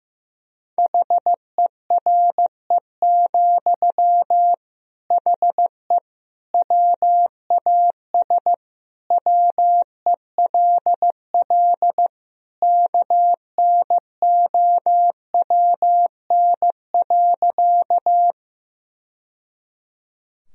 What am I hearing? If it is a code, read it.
Morse code, heard as a keyed tone, about 15 wpm, 700 Hz. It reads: HERE, HE WAS WELL KNOWN.
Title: Sentences from Top 1000 Words - Review Part 3 - 15wpm